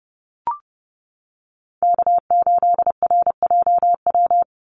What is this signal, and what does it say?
Morse code, heard as a keyed tone, about 30 wpm, 700 Hz. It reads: K8RJW
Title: Call Signs - 30wpm